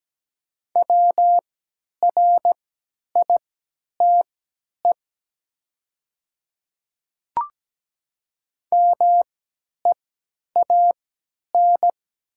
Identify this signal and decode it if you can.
Morse code, heard as a keyed tone, about 17 wpm, spaced as for 10 wpm, 700 Hz. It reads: WRITE MEAN
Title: 3x Character Spacing - Top 200 Words - 17wpm